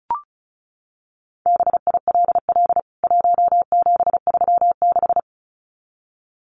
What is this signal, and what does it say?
Morse code, heard as a keyed tone, about 34 wpm, 700 Hz. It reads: BILL 1736
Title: CWT Exchange 35wpm